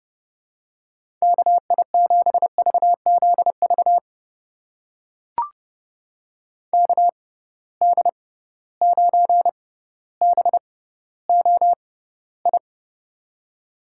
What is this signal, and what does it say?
Morse code, heard as a keyed tone, about 32 wpm, spaced as for 16 wpm, 700 Hz. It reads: KI7VZV KD9BOI
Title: USA Call Signs - 6 Characters - 6x Character Spacing - 30wpm